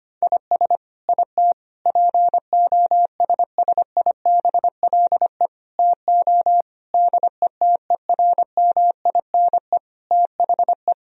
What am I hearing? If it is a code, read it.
Morse code, heard as a keyed tone, about 25 wpm, 700 Hz. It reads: IS IT POSSIBLE TO DETERMINE THE
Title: Sentences from Top 1000 Words - Review Part 4 - 25wpm